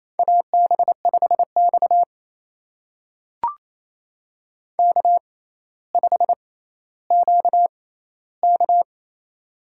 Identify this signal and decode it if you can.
Morse code, heard as a keyed tone, about 27 wpm, spaced as for 14 wpm, 700 Hz. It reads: AB5X K5QK
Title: USA Call Signs - 4 Characters - 6x Character Spacing - 28wpm